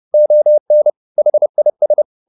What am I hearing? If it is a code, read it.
Morse code, heard as a keyed tone, about 30 wpm, 600 Hz. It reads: ON HIS